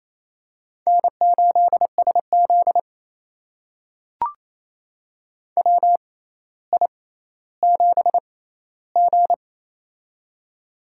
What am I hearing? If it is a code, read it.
Morse code, heard as a keyed tone, about 28 wpm, spaced as for 14 wpm, 700 Hz. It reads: N8SZ WI7G
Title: USA Call Signs - 4 Characters - 6x Character Spacing - 28wpm